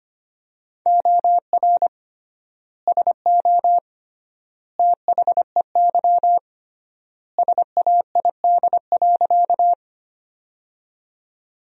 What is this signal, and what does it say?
Morse code, heard as a keyed tone, about 25 wpm, 700 Hz. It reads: OR SO THEY SAID.
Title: 3x Word-Spacing - Sentences from Top 100 Words - 25wpm